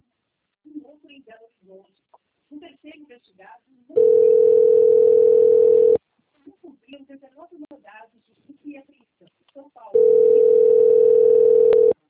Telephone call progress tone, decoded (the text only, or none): ringback tone